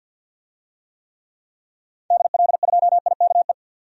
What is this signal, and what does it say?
Morse code, heard as a keyed tone, about 51 wpm, 700 Hz. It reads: DL2IKE
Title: DXCC-100 Call Signs - 6 Characters - 50wpm